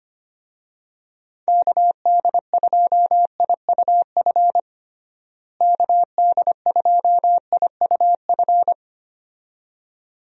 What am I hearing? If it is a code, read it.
Morse code, heard as a keyed tone, about 25 wpm, 700 Hz. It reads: KD2IUF KD2IUF
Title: USA Call Signs - 6 Characters - Repeated 2x - 25wpm